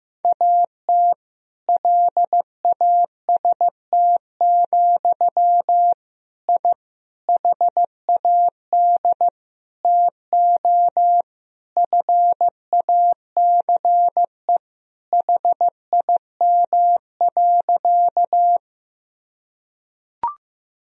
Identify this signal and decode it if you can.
Morse code, heard as a keyed tone, about 15 wpm, 700 Hz. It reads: AT LAST, I HAD TO FACE HIM.
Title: Sentences from Top 1000 Words - Review Part 2 - 15wpm